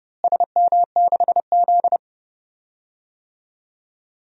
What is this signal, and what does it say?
Morse code, heard as a keyed tone, about 30 wpm, 700 Hz. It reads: SM6Z
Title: DXCC-100 Call Signs - 4 Characters - Repeated 2x - 30wpm